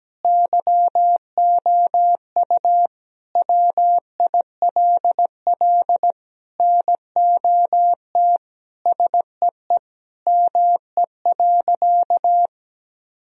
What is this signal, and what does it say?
Morse code, heard as a keyed tone, about 17 wpm, 700 Hz. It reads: YOU WILL NOT SEE ME.
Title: Sentences from Top 1000 Words - Review Part 1 - 17wpm